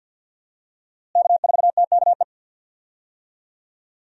Morse code, heard as a keyed tone, about 53 wpm, 700 Hz. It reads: K4TKE
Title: USA Call Signs - 5 Characters - Repeated 3x - 50wpm